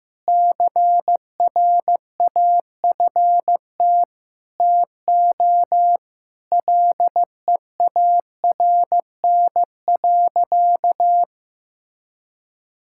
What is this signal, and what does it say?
Morse code, heard as a keyed tone, about 16 wpm, 700 Hz. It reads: CRAFT TO LEARN.
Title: Sentences from Top 1000 Words - Review Part 6 - 15wpm